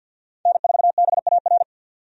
Morse code, heard as a keyed tone, about 51 wpm, 700 Hz. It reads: N4BAR
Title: USA Call Signs - 5 Characters - 50wpm